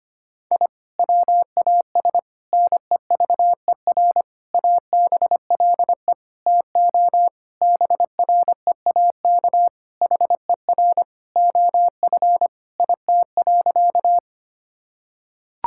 Morse code, heard as a keyed tone, about 25 wpm, 700 Hz. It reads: I WAS NEVER ABLE TO BREAK HER OF IT.
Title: Sentences from Top 1000 Words - Review Part 6 - 25wpm